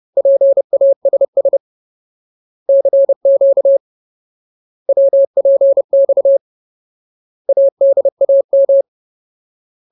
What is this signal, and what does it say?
Morse code, heard as a keyed tone, about 30 wpm, 550 Hz. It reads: PASS CQ WPX ADAM
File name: Ultimate Word List / 30WPM Ultimate